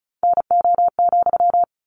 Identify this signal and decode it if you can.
Morse code, heard as a keyed tone, about 34 wpm, 700 Hz. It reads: NO,
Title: Sentences from Top 600 Words 35wpm